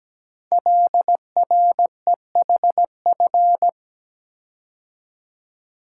Morse code, heard as a keyed tone, about 17 wpm, 700 Hz. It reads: LREHF
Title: Code Groups - 5 Letters - 17wpm